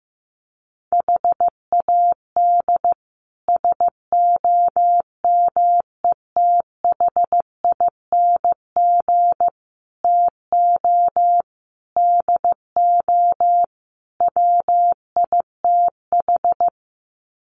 Morse code, heard as a keyed tone, about 15 wpm, 700 Hz. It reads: HAD SOMETHING TO DO WITH
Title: Sets of 5 Words 15wpm